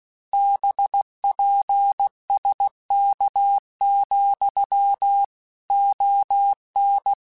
Morse code, heard as a keyed tone, about 16 wpm, 800 Hz. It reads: BPSK, ON